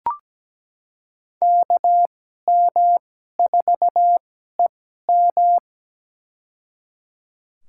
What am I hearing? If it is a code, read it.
Morse code, heard as a keyed tone, about 17 wpm, spaced as for 12 wpm, 700 Hz. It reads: KM4EM
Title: USA Call Signs - 5 Characters - 2x Character Spacing - 17wpm